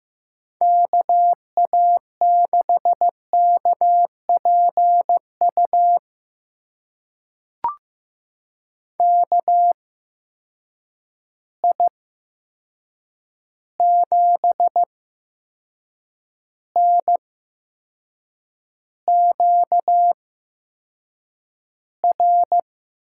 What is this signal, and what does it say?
Morse code, heard as a keyed tone, about 15 wpm, spaced as for 6 wpm, 700 Hz. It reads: KA6KPU KI7NQR
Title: USA Call Signs - 6 Characters - 8x Character Spacing - 15wpm